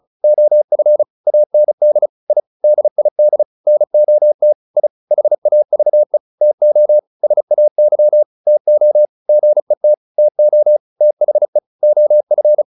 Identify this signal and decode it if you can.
Morse code, heard as a keyed tone, about 34 wpm, 600 Hz. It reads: OF AND I DID NOT I HAVE TO SAY TO GET TO THE OF